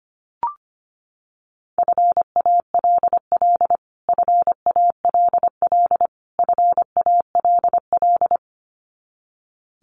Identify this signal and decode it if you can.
Morse code, heard as a keyed tone, about 25 wpm, 700 Hz. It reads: FALL FALL FALL
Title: Top 500 Words Repeated 25wpm